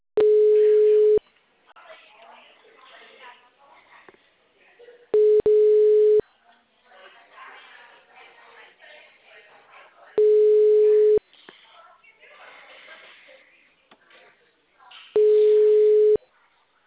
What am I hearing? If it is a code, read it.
Ringback tone (Europe, 425 Hz), with speech.
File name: out- -2003- - - .31751